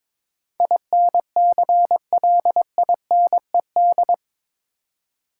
Morse code, heard as a keyed tone, about 22 wpm, 700 Hz. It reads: INCLINED
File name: practice / 22wpm / Eight Letter Words - 22wpm